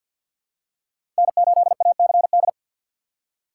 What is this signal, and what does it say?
Morse code, heard as a keyed tone, about 52 wpm, 700 Hz. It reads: N8ACD
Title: USA Call Signs - 5 Characters - 2x Character Spacing - 50wpm